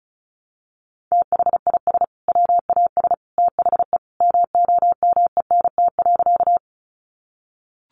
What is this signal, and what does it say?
Morse code, heard as a keyed tone, about 34 wpm, 700 Hz. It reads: THIS WAS THE MOMENT.
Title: Sentences from Top 1000 Words 35wpm